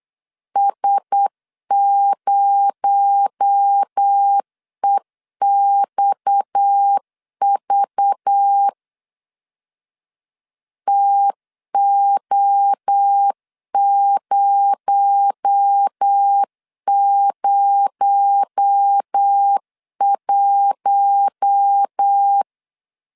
Morse code, heard as a keyed tone, about 8 wpm, 800 Hz. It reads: S0EXV TO001
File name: LEZIONI / 6 PKX0 / 6 PKX0 8wpm 3341